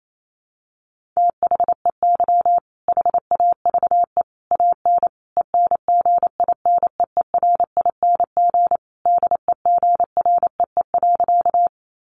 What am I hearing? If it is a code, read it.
Morse code, heard as a keyed tone, about 28 wpm, 700 Hz. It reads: THEY HAVE AN ENGINEERING DEGREE.
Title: Sentences from Top 1500 Words 28wpm